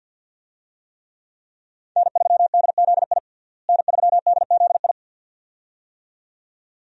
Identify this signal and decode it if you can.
Morse code, heard as a keyed tone, about 52 wpm, 700 Hz. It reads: N3DZI N3DZI
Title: USA Call Signs - 5 Characters - Repeated 2x - 50wpm